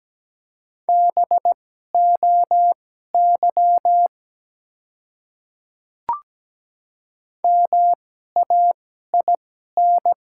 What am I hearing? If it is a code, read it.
Morse code, heard as a keyed tone, about 17 wpm, spaced as for 12 wpm, 700 Hz. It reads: BOY MAIN
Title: 2x Character Spacing - Top 300 Words - 17wpm